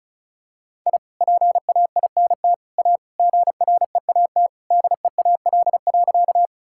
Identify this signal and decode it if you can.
Morse code, heard as a keyed tone, about 34 wpm, 700 Hz. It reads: I PAINT A GREAT DEAL.
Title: Sentences from Top 1000 Words - Review Part 6 - 35wpm